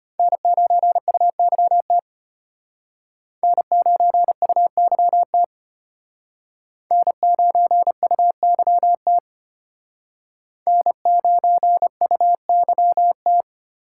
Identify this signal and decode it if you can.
Morse code, heard as a keyed tone, about 32 wpm, 700 Hz. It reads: N9UYT N9UYT N9UYT N9UYT